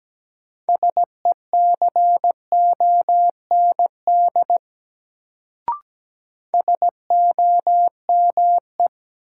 Morse code, heard as a keyed tone, about 17 wpm, 700 Hz. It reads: SECOND SOME
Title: Sets of 2 Words - New Word Drill - 17wpm